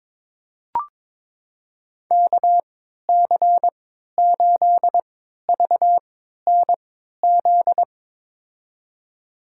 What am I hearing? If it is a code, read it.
Morse code, heard as a keyed tone, about 22 wpm, spaced as for 12 wpm, 700 Hz. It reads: KC8VNZ
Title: USA Call Signs - 3x Character Spacing - 22wpm